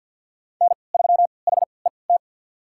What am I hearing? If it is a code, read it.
Morse code, heard as a keyed tone, about 47 wpm, spaced as for 28 wpm, 700 Hz. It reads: N3HET
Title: USA Call Signs - 5 Characters - 3x Character Spacing - 50wpm